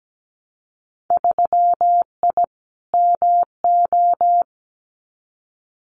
Morse code, heard as a keyed tone, about 17 wpm, 700 Hz. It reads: 3I MO
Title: Winter Field Day Exchanges 17wpm